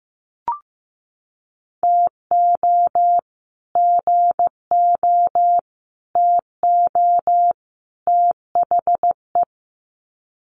Morse code, heard as a keyed tone, about 14 wpm, 700 Hz. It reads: TO GO TO THE
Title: Sets of 4 Words 15wpm